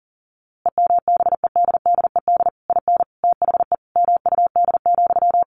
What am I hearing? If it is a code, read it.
Morse code, heard as a keyed tone, about 40 wpm, 700 Hz. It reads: EMBEDDED IN THE MUD,
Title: A Sound Of Thunder 40wpm